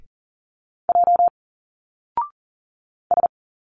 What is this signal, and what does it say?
Morse code, heard as a keyed tone, about 40 wpm, 700 Hz. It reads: J S
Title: Single Letters 40wpm